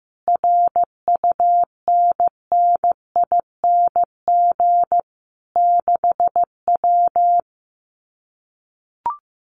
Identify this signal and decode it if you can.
Morse code, heard as a keyed tone, about 15 wpm, 700 Hz. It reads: RUNNING 6W
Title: Common QSO Phrases Encore 15wpm